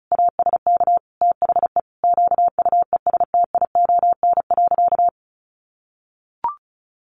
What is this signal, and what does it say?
Morse code, heard as a keyed tone, about 34 wpm, 700 Hz. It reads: ASK THE QUESTION.
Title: Sentences from Top 600 Words 35wpm